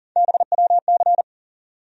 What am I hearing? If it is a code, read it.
Morse code, heard as a keyed tone, about 40 wpm, 700 Hz. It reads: BWC
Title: USA Call Sign Postfixes - 3 Letters 40wpm